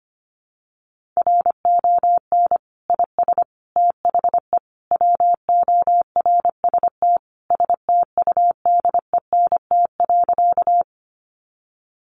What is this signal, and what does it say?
Morse code, heard as a keyed tone, about 25 wpm, 700 Hz. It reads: RON IS THE WORST STUDENT.
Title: Additional Easy Phrases 25wpm